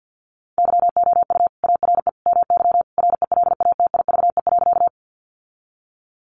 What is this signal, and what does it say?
Morse code, heard as a keyed tone, about 48 wpm, 700 Hz. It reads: YOU ARE MY RELATIVE.
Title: Sentences from Top 1400 Words 50wpm